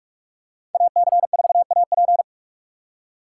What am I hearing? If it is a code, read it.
Morse code, heard as a keyed tone, about 45 wpm, 700 Hz. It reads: AC4AP